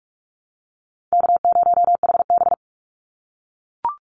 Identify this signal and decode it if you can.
Morse code, heard as a keyed tone, about 44 wpm, 700 Hz. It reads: K0HB
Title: Call Signs - 45wpm